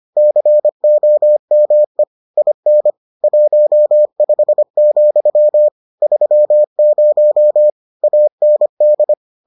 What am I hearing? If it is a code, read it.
Morse code, heard as a keyed tone, about 25 wpm, 600 Hz. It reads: COME IN 15, 30 AND